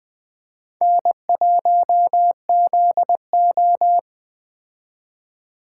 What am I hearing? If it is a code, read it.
Morse code, heard as a keyed tone, about 21 wpm, 700 Hz. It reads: N1ZO